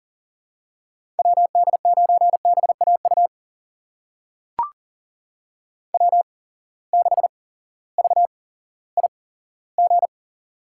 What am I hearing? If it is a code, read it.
Morse code, heard as a keyed tone, about 38 wpm, spaced as for 18 wpm, 700 Hz. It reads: WD9BAU W6VIG
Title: USA Call Signs - 8x Character Spacing - 40wpm